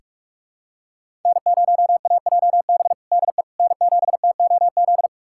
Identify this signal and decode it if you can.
Morse code, heard as a keyed tone, about 47 wpm, 700 Hz. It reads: N0AJB DE N7TOZ
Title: USA Callsign DE Callsign - 5 Characters - 45wpm